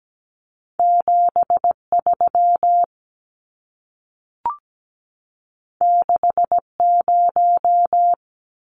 Morse code, heard as a keyed tone, about 17 wpm, 700 Hz. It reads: 73 60
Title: Numbers - 2-Digit 17wpm